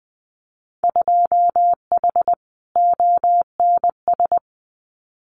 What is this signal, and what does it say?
Morse code, heard as a keyed tone, about 20 wpm, 700 Hz. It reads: 2H ONS
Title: Winter Field Day Exchanges Encore 20wpm